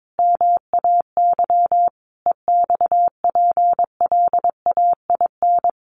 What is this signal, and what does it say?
Morse code, heard as a keyed tone, about 22 wpm, 700 Hz. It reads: MAY EXPLAIN
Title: Sentences from Top 1100 Words 22wpm